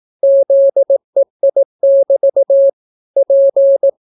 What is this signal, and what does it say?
Morse code, heard as a keyed tone, about 18 wpm, 550 Hz. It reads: ZEI<BT> P